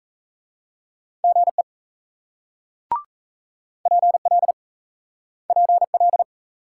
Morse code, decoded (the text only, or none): GE PL PL